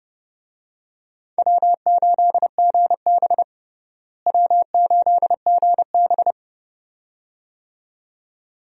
Morse code, heard as a keyed tone, about 30 wpm, 700 Hz. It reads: W8GB W8GB
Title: USA Call Signs - 4 Characters - Repeated 2x - 30wpm